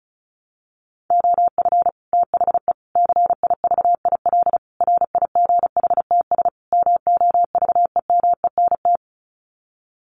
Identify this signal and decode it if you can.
Morse code, heard as a keyed tone, about 34 wpm, 700 Hz. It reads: OF THE CIVIL RIGHTS MOVEMENT
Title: Sets of 5 Words Encore2 35wpm